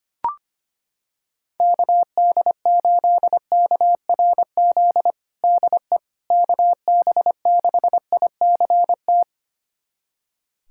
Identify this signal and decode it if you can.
Morse code, heard as a keyed tone, about 25 wpm, 700 Hz. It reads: KD8KRZ DE KB6ICT
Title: USA Callsign DE Callsign 25wpm